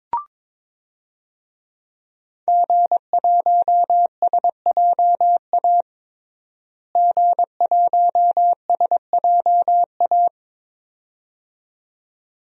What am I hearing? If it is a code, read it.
Morse code, heard as a keyed tone, about 22 wpm, 700 Hz. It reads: G1SJA G1SJA
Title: DXCC-100 Call Signs - 5 Characters - Repeated 2x - 22wpm